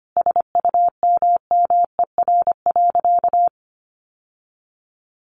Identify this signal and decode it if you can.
Morse code, heard as a keyed tone, about 25 wpm, 700 Hz. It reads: SUMMER.
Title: Sentences from Top 1100 Words 25wpm